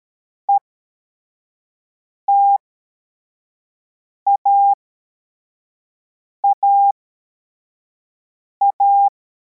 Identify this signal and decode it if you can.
Morse code, heard as a keyed tone, about 13 wpm, spaced as for 4 wpm, 800 Hz. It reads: ETAAA